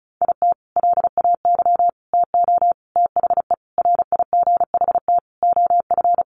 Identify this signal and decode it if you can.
Morse code, heard as a keyed tone, about 34 wpm, 700 Hz. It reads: IT LAY TO THE RIGHT OF
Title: Sentences from Top 1400 Words 35wpm